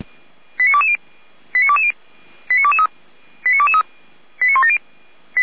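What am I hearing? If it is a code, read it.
Five-tone selective call (ZVEI), heard as quick runs of five tones, tones of about 70 ms. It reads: 89220 89220 89223 89223 89170
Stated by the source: Five-tone ZVEI selective call received by SDR, from Feuerwehr Orlamünde audio archive.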